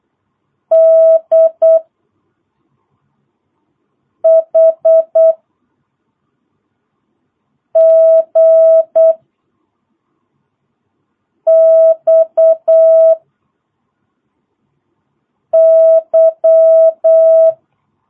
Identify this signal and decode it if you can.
Morse code, heard as a keyed tone, about 8 wpm, 650 Hz. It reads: D H G X Y
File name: jh8cbh / 2022cw